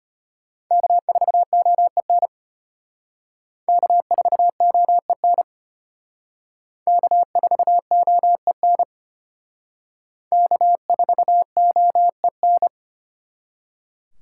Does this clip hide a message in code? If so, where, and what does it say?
Yes: Morse code from 0.7 to 12.7 s, heard as a keyed tone, about 34 wpm, 700 Hz. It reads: K4OEN K4OEN K4OEN K4OEN